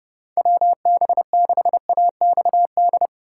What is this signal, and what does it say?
Morse code, heard as a keyed tone, about 31 wpm, 700 Hz. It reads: WB6AXD